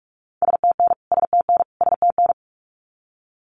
Morse code, heard as a keyed tone, about 44 wpm, 700 Hz. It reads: STN STN STN
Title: Abbreviations Repeated 3x - 45wpm